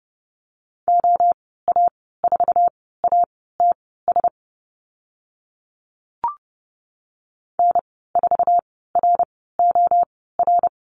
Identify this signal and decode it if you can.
Morse code, heard as a keyed tone, about 30 wpm, spaced as for 17 wpm, 700 Hz. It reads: OA4ATS N4ROR
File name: Lesson 14g - Number 2 - Call Signs 30wpm